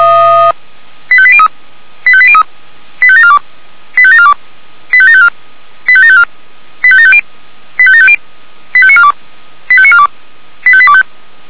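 Five-tone selective call (ZVEI), heard as quick runs of five tones, tones of about 70 ms. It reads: 85803 85803 85742 85742 85754 85754 85750 85750 85942 85942 85825
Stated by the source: ZVEI five-tone selective call received with an SDR.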